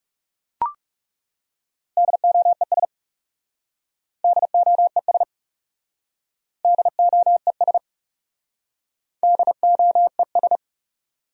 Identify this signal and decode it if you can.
Morse code, heard as a keyed tone, about 39 wpm, 700 Hz. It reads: DOES DOES DOES DOES